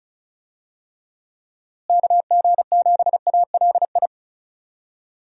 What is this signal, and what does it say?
Morse code, heard as a keyed tone, about 36 wpm, 700 Hz. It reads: KG7ALI
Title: USA Call Signs - 6 Characters - Repeated 2x - 35wpm